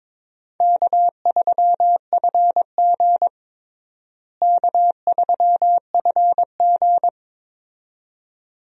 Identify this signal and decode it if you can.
Morse code, heard as a keyed tone, about 22 wpm, 700 Hz. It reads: K3FG K3FG
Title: USA Call Signs - 4 Characters - Repeated 2x - 22wpm